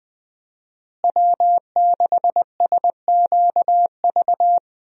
Morse code, heard as a keyed tone, about 20 wpm, 700 Hz. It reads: W6SQV
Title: USA Call Signs - 5 Characters - Repeated 3x - 20wpm